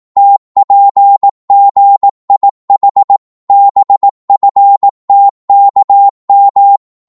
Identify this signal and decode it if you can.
Morse code, heard as a keyed tone, about 18 wpm, 800 Hz. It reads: TPGIH BFTKM
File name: Morse / ltr-18wpm